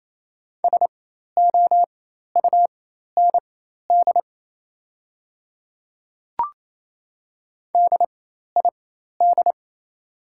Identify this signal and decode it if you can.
Morse code, heard as a keyed tone, about 28 wpm, spaced as for 13 wpm, 700 Hz. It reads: SOUND DID
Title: 4x Character Spacing - Top 100 Words - 28wpm